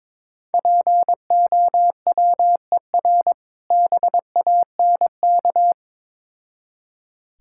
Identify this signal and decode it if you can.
Morse code, heard as a keyed tone, about 22 wpm, 690 Hz. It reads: POWER BANK
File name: Difficult Words with Hints - 22wpm